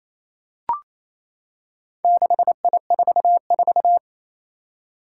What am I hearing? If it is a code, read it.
Morse code, heard as a keyed tone, about 27 wpm, 700 Hz. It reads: 6I44
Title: Four Challenging Letter-Number Combinations 28wpm